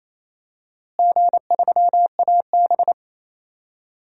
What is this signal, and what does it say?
Morse code, heard as a keyed tone, about 28 wpm, 700 Hz. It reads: G3AB
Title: DXCC-100 Call Signs - 4 Characters - Repeated 3x - 28wpm